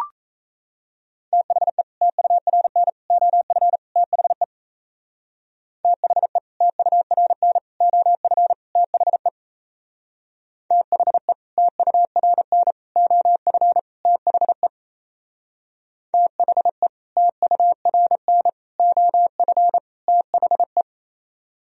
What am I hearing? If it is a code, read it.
Morse code, heard as a keyed tone, about 37 wpm, 700 Hz. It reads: THE TURN OF THE THE TURN OF THE THE TURN OF THE THE TURN OF THE